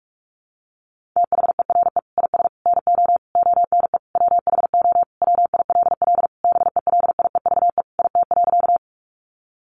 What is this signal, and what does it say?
Morse code, heard as a keyed tone, about 44 wpm, 700 Hz. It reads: THERE IS NO ONE WHO WILL BELIEVE IT.
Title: Sentences from Top 1000 Words 45wpm